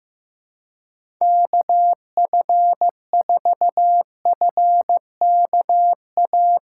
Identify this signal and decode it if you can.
Morse code, heard as a keyed tone, about 15 wpm, 700 Hz. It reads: KF4FKA
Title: USA Call Signs - 6 Characters - 8x Character Spacing - 15wpm